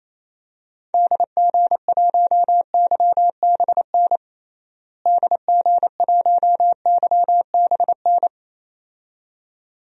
Morse code, heard as a keyed tone, about 28 wpm, 700 Hz. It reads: DG1YBN DG1YBN
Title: DXCC-100 Call Signs - 6 Characters - Repeated 2x - 28wpm